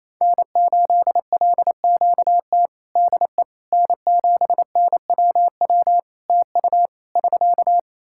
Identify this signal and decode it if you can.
Morse code, heard as a keyed tone, about 28 wpm, 700 Hz. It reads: N8LQT DE N7NWW TU <SK>
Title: Practice QSO - Exchange 4 - Ending - 28wpm